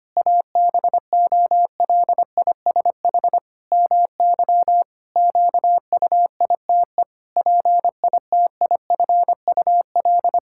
Morse code, heard as a keyed tone, about 25 wpm, 700 Hz. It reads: ABOLISH MY QUITE PITIFUL
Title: Pangram Challenge v2 - 25wpm